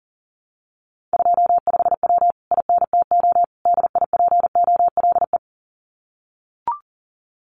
Morse code, heard as a keyed tone, about 40 wpm, 700 Hz. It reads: 25W INTO DIPOLE
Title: Common QSO Phrases Encore 40wpm